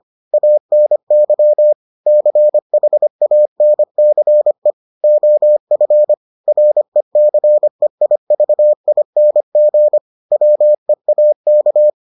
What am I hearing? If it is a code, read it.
Morse code, heard as a keyed tone, about 25 wpm, 600 Hz. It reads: ANY CHANCE OF RECEIVING WEAK